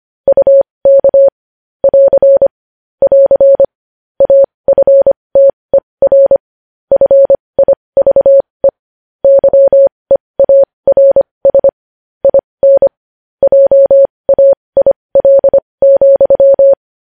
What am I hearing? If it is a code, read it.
Morse code, heard as a keyed tone, about 25 wpm, spaced as for 20 wpm, 550 Hz. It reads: UK <AR> <AR> AFTER FIVE YEARS IN JAIL,
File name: podcast / News headlines-1x-25x20